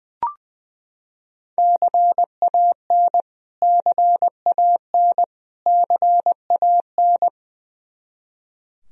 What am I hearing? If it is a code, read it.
Morse code, heard as a keyed tone, about 20 wpm, 700 Hz. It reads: CAN CAN CAN
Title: Top 100 Words Repeated 20wpm